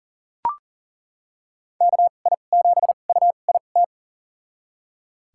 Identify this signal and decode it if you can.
Morse code, heard as a keyed tone, about 40 wpm, spaced as for 29 wpm, 700 Hz. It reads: KI7UIT
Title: USA Call Signs - 6 Characters - 2x Character Spacing - 40wpm